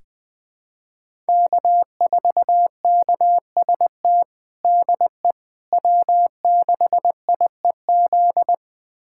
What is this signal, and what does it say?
Morse code, heard as a keyed tone, about 20 wpm, 700 Hz. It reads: K4KST DE W6IEZ
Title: USA Callsign DE Callsign - 5 Characters - 20wpm